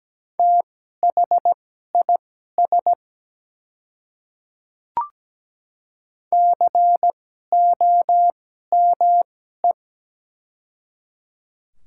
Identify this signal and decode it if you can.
Morse code, heard as a keyed tone, about 17 wpm, spaced as for 12 wpm, 700 Hz. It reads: THIS COME
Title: 2x Character Spacing - Top 100 Words - 17wpm